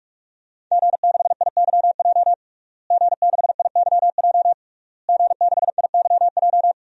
Morse code, heard as a keyed tone, about 41 wpm, 700 Hz. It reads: G6IYJ G6IYJ G6IYJ
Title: DXCC-100 Call Signs - 5 Characters - Repeated 3x - 45wpm